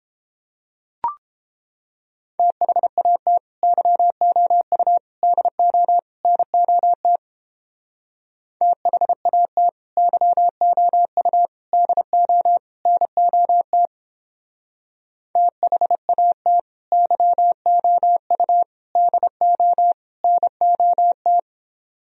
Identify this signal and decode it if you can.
Morse code, heard as a keyed tone, about 29 wpm, 700 Hz. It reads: THAT YOU DO NOT THAT YOU DO NOT THAT YOU DO NOT